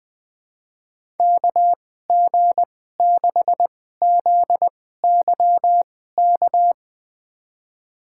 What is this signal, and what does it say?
Morse code, heard as a keyed tone, about 20 wpm, spaced as for 14 wpm, 700 Hz. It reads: KG6ZYK
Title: USA Call Signs - 6 Characters - 2x Character Spacing - 20wpm